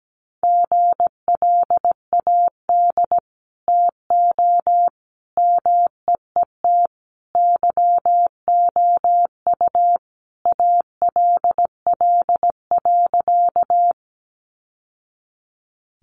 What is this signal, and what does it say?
Morse code, heard as a keyed tone, about 17 wpm, 700 Hz. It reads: GLAD TO MEET YOU ALL.
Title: Sentences from Top 1100 Words 17wpm